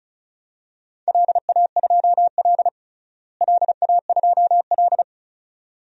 Morse code, heard as a keyed tone, about 36 wpm, 700 Hz. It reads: LA2L LA2L
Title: DXCC-100 Call Signs - 4 Characters - Repeated 2x - 35wpm